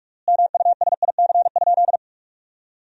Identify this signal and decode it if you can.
Morse code, heard as a keyed tone, about 43 wpm, 700 Hz. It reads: MUSIC?
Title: 3x Word-Spacing - Sentences from Top 300 Words - 45wpm